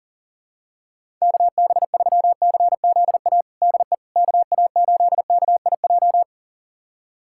Morse code, heard as a keyed tone, about 40 wpm, 700 Hz. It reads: KB3CZA DE KA8KIJ
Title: USA Callsign DE Callsign - 6 Characters - 40wpm